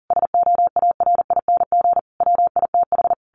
Morse code, heard as a keyed tone, about 40 wpm, 700 Hz. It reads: SOARING WITH
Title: A Sound Of Thunder 40wpm